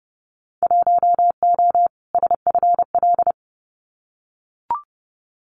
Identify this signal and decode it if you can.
Morse code, heard as a keyed tone, about 30 wpm, 700 Hz. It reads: 1O SFL
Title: Winter Field Day Exchanges 30wpm